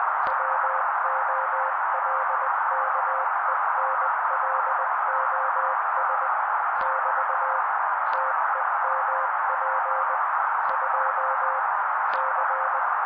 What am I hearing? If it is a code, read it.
Morse code, heard as a keyed tone, about 20 wpm, 530 Hz. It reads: WOLKENLOS <BT> TEMP 2 C
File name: daily / 18wpm noise